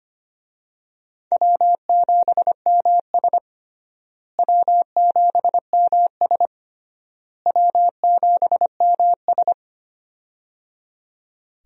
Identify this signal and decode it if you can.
Morse code, heard as a keyed tone, about 25 wpm, 700 Hz. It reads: W7MS W7MS W7MS
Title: USA Call Signs - 4 Characters - Repeated 3x - 25wpm